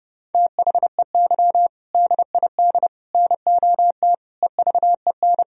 Morse code, heard as a keyed tone, about 30 wpm, 700 Hz. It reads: THEY DID NOT EVEN
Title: Sentences from Top 1000 Words - Review Part 1 - 30wpm